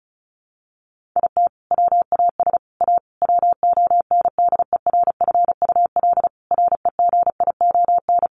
Morse code, heard as a keyed tone, about 34 wpm, 700 Hz. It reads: IT WAS A WONDERFUL REGION
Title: Sentences from Top 1500 Words 35wpm